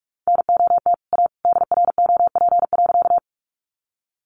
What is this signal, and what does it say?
Morse code, heard as a keyed tone, about 44 wpm, 700 Hz. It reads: NOT A DROP.
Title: Sentences from Top 1400 Words 45wpm